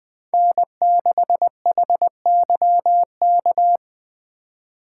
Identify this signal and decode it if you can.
Morse code, heard as a keyed tone, about 20 wpm, 700 Hz. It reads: N6HYK
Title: Speed Racing - USA 1x3 Call Signs 17wpm